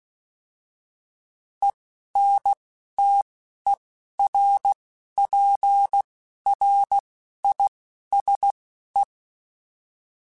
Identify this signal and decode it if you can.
Morse code, heard as a keyed tone, about 16 wpm, spaced as for 11 wpm, 800 Hz. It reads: ENTERPRISE
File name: fromSMs / FromSMs mp3s / Cities@10